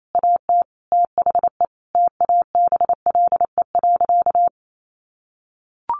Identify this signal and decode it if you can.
Morse code, heard as a keyed tone, about 28 wpm, 700 Hz. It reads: AT THE TABLE.
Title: Sentences from Top 1500 Words 28wpm